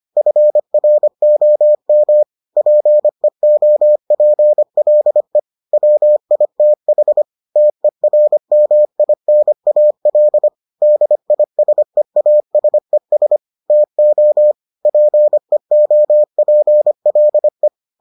Morse code, heard as a keyed tone, about 25 wpm, 600 Hz. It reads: FROM PEOPLE WITH TERMINAL DISEASES TO PEOPLE